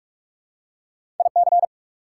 Morse code, heard as a keyed tone, about 46 wpm, 690 Hz. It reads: IC
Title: Prefixes and Suffixes 45wpm